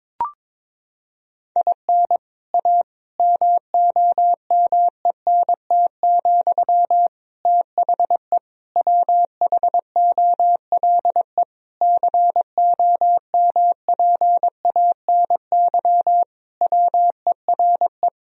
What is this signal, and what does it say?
Morse code, heard as a keyed tone, about 22 wpm, 700 Hz. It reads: IN A MOMENT, THE WHOLE COMPANY WERE
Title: Sentences from Top 1000 Words - Review Part 3 - 22wpm